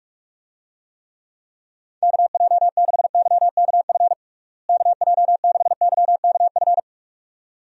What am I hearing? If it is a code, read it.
Morse code, heard as a keyed tone, about 48 wpm, 700 Hz. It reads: KJ6YKF KJ6YKF
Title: USA Call Signs - 6 Characters - Repeated 2x - 45wpm